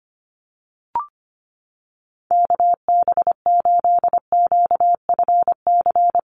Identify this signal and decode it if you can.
Morse code, heard as a keyed tone, about 25 wpm, 700 Hz. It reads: KB8QFC
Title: USA Call Signs Repeated 2x 25wpm